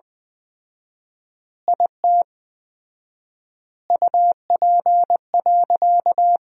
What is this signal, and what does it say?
Morse code, heard as a keyed tone, about 20 wpm, 700 Hz. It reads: IT UP.
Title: 4x Word-Spacing - Sentences from Top 100 Words - 20wpm